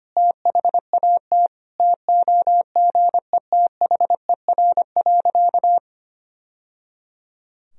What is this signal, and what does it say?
Morse code, heard as a keyed tone, about 25 wpm, 700 Hz. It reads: THAT TOGETHER.
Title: Sentences from Top 1000 Words - Review Part 1 - 25wpm